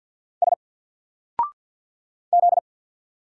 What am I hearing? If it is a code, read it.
Morse code, heard as a keyed tone, about 48 wpm, 700 Hz. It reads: S Z
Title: Single Letter-Number Alt 50wpm